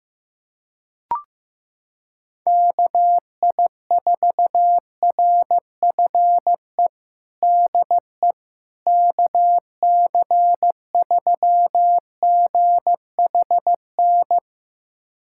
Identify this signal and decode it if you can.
Morse code, heard as a keyed tone, about 15 wpm, 700 Hz. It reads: KI4RFE DE KC3GHN
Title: USA Callsign DE Callsign - 6 Characters - 15wpm